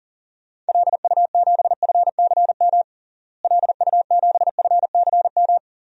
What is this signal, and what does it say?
Morse code, heard as a keyed tone, about 40 wpm, 700 Hz. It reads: LU7FCM LU7FCM
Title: DXCC-100 Call Signs - 6 Characters - Repeated 2x - 40wpm